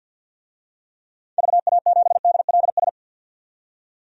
Morse code, heard as a keyed tone, about 52 wpm, 700 Hz. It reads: VA7DLS
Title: DXCC-100 Call Signs - 6 Characters - 50wpm